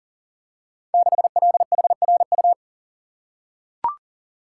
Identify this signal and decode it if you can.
Morse code, heard as a keyed tone, about 43 wpm, 700 Hz. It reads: 6LHRU
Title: Code Groups - 5 Letter:Numbers - 40wpm